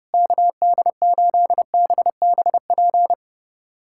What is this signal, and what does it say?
Morse code, heard as a keyed tone, about 30 wpm, 700 Hz. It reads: KD8BBP
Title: USA Callsign DE Callsign 30wpm